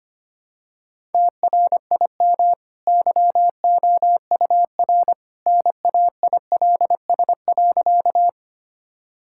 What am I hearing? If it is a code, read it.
Morse code, heard as a keyed tone, about 25 wpm, 700 Hz. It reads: TRIM YOUR NAILS.
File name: Easy Sentences with Hints - 25wpm